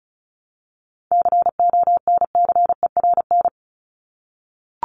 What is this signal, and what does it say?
Morse code, heard as a keyed tone, about 34 wpm, 700 Hz. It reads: CONCERN
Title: ING Suffix - Top 100 Words with Base Words 35wpm